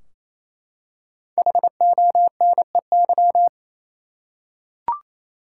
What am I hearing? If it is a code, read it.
Morse code, heard as a keyed tone, about 28 wpm, 700 Hz. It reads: HONEY
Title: Five Letter Words - 28wpm